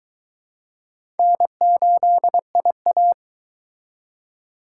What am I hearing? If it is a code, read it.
Morse code, heard as a keyed tone, about 22 wpm, 700 Hz. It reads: N8IA